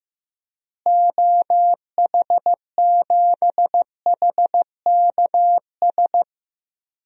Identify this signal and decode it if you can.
Morse code, heard as a keyed tone, about 15 wpm, 700 Hz. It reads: OH7HKS
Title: DXCC-100 Call Signs - 6 Characters - Repeated 3x - 15wpm